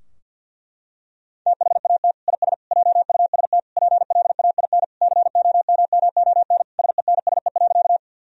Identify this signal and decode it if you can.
Morse code, heard as a keyed tone, about 52 wpm, 700 Hz. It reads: THAT IS JUST PLAIN COMMON SENSE.
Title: Sentences from Top 1000 Words - Review Part 4 - 50wpm